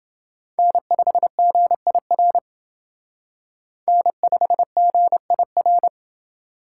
Morse code, heard as a keyed tone, about 30 wpm, 700 Hz. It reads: N5GIR N5GIR